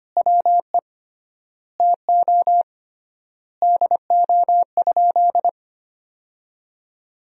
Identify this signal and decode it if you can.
Morse code, heard as a keyed tone, about 25 wpm, 700 Hz. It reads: WE TO DO?
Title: 3x Word-Spacing - Sentences from Top 100 Words - 25wpm